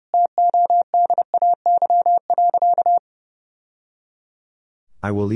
Morse code, heard as a keyed tone, about 29 wpm, 700 Hz. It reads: TODAY.
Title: Sentences from Top 1000 Words - Review Part 3 - 30wpm